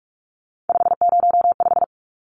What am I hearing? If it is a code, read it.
Morse code, heard as a keyed tone, about 44 wpm, 700 Hz. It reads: 505
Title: Numbers - 3-Digit 45wpm